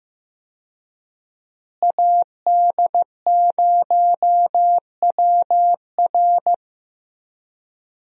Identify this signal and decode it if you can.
Morse code, heard as a keyed tone, about 15 wpm, 700 Hz. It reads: AD0WR
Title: USA Call Signs - 5 Characters - Repeated 3x - 15wpm